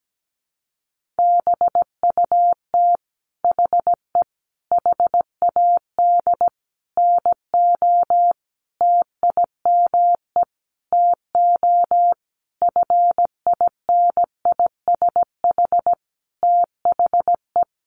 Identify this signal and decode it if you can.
Morse code, heard as a keyed tone, about 17 wpm, 700 Hz. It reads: BUT HE HAD NO TIME TO FINISH THE
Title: Sentences from Top 1200 Words 17wpm